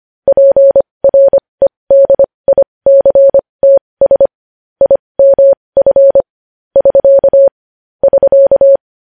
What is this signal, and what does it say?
Morse code, heard as a keyed tone, about 25 wpm, spaced as for 20 wpm, 550 Hz. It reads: PREDICTS IMF <SK> <SK>
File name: podcast / News headlines-1x-25x20